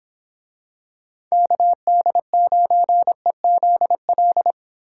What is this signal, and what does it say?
Morse code, heard as a keyed tone, about 25 wpm, 700 Hz. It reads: KD9EZL